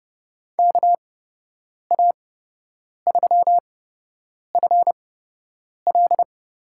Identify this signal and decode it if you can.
Morse code, heard as a keyed tone, about 31 wpm, spaced as for 8 wpm, 700 Hz. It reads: KA3FL
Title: USA Call Signs - 6 Characters - 8x Character Spacing - 30wpm